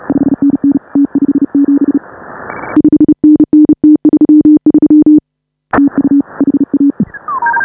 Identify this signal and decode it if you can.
Morse code, heard as a keyed tone, about 36 wpm, 310 Hz. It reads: 5NN T57 5NN T33 TU HA